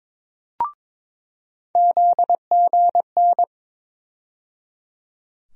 Morse code, heard as a keyed tone, about 22 wpm, 700 Hz. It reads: ZGN
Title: USA Call Sign Postfixes - 3 Letters 22wpm